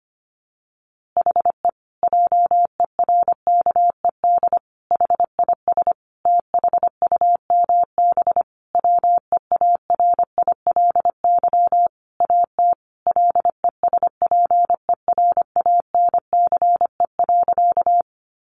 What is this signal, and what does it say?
Morse code, heard as a keyed tone, about 25 wpm, 700 Hz. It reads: HE JERKED HIS THUMB WEARILY AT LESPERANCE.
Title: A Sound Of Thunder 25wpm